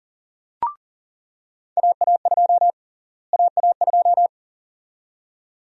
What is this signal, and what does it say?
Morse code, heard as a keyed tone, about 40 wpm, 700 Hz. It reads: AA2 AA2
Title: USA Call Sign Prefixes - Repeated 2x - 40wpm